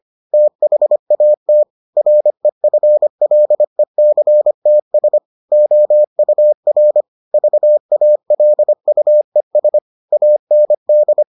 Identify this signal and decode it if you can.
Morse code, heard as a keyed tone, about 25 wpm, 600 Hz. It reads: THAT REFLECTS OUR VALUES AND